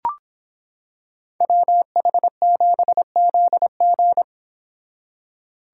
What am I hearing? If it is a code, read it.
Morse code, heard as a keyed tone, about 26 wpm, 700 Hz. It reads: WH7ZG